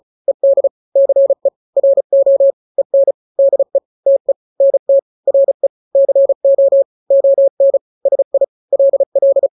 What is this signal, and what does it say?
Morse code, heard as a keyed tone, about 35 wpm, 550 Hz. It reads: ED CE RO EN DE TE NT RE CO ON SI LL
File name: n-grams / 2-grams 35 30 WPM